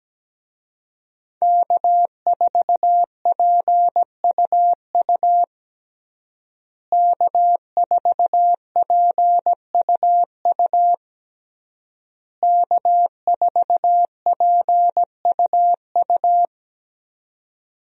Morse code, heard as a keyed tone, about 17 wpm, 700 Hz. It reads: K4PUU K4PUU K4PUU
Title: USA Call Signs - 5 Characters - Repeated 3x - 17wpm